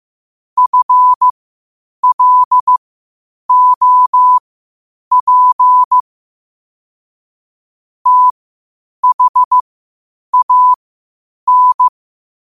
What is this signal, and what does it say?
Morse code, heard as a keyed tone, about 15 wpm, spaced as for 8 wpm, 1000 Hz. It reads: FLOP THAN
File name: qotdfiles / 8wpm